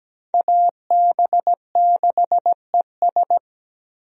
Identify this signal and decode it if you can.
Morse code, heard as a keyed tone, about 17 wpm, 700 Hz. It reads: AB6ES
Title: USA Call Signs - 5 Characters - Repeated 2x - 17wpm